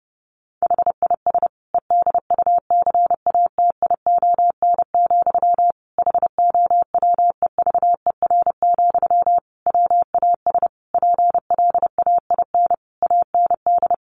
Morse code, heard as a keyed tone, about 30 wpm, 700 Hz. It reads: HIS EDUCATION, HOWEVER, WAS PLAIN AND
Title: Sentences from Top 1400 Words 30wpm